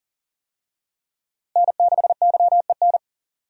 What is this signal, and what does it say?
Morse code, heard as a keyed tone, about 40 wpm, 700 Hz. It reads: N6YEN
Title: USA Call Signs - 5 Characters - Repeated 3x - 40wpm